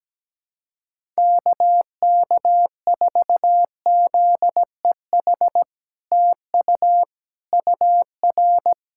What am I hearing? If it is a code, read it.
Morse code, heard as a keyed tone, about 17 wpm, 700 Hz. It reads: KK4ZEH TU UR
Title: SOTA QSO Practice 17wpm